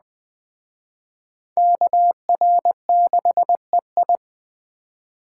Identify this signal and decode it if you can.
Morse code, heard as a keyed tone, about 21 wpm, 700 Hz. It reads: KR6EI
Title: Speed Racing - USA Call Signs 17wpm